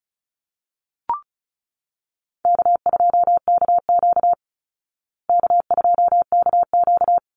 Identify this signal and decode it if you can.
Morse code, heard as a keyed tone, about 34 wpm, 700 Hz. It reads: K2KQ K2KQ
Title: Easy Call Signs Repeated 2x - 35wpm